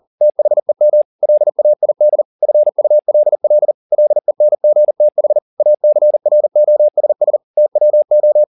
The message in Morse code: THEM LAID FULL LENGTH ACROSS TWO

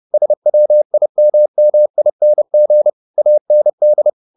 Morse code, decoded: SWIMMING AND

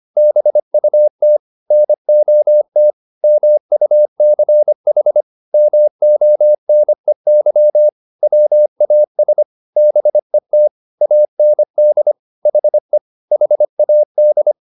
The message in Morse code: BUT NOT MUCH MONEY WAS BET AND HE HAD

